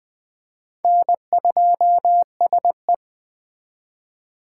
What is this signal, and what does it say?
Morse code, heard as a keyed tone, about 20 wpm, 700 Hz. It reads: N2SE
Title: Speed Racing - USA 1x2 Call Signs 17wpm